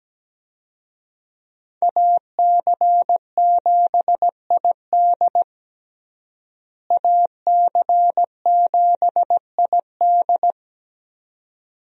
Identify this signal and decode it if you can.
Morse code, heard as a keyed tone, about 17 wpm, 700 Hz. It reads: AC7ID AC7ID
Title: USA Call Signs - 5 Characters - Repeated 2x - 17wpm